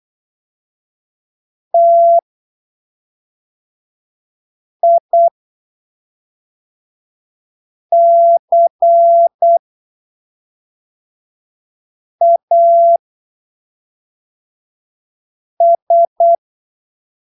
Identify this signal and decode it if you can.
Morse code, heard as a keyed tone, about 8 wpm, spaced as for 3 wpm, 680 Hz. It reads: TICAS
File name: arquivos / texto / texto 13 08ppm